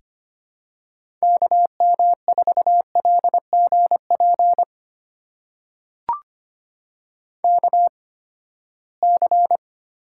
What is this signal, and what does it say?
Morse code, heard as a keyed tone, about 25 wpm, spaced as for 17 wpm, 700 Hz. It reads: KM4LGP KC
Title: USA Call Signs - 6 Characters - 8x Character Spacing - 25wpm